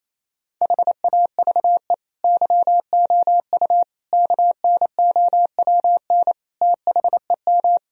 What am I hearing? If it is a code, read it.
Morse code, heard as a keyed tone, about 27 wpm, 700 Hz. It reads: HAVE YOU KNOWN THEM
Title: Sentences from Top 1000 Words - Review Part 3 - 28wpm